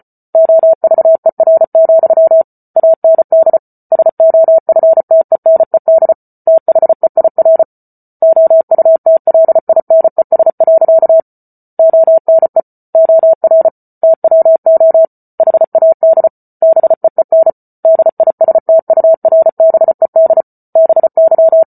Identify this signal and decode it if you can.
Morse code, heard as a keyed tone, about 34 wpm, 650 Hz. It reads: OVER, AND SOFTENED THEIR OUTLINES. ONE OR TWO HAD BEEN DISTURBED BY